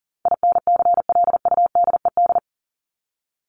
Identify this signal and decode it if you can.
Morse code, heard as a keyed tone, about 40 wpm, 700 Hz. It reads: INCLUDED
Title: ED Suffix - Top 100 Words with Base Words 40wpm